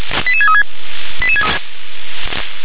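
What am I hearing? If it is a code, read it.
Five-tone selective call (ZVEI), heard as quick runs of five tones, tones of about 70 ms. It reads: 88636
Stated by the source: ZVEI five-tone call received by SDR.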